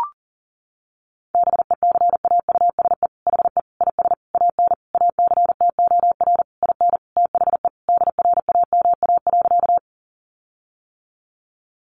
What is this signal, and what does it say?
Morse code, heard as a keyed tone, about 40 wpm, 700 Hz. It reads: BECAUSE HE IS AN ACTOR IN THE DRAMA.
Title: Sentences from Top 1500 Words 40wpm